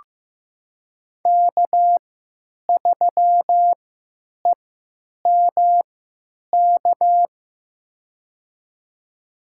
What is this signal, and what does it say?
Morse code, heard as a keyed tone, about 16 wpm, spaced as for 9 wpm, 700 Hz. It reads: K3EMK
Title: USA Call Signs - 5 Characters - 3x Character Spacing - 15wpm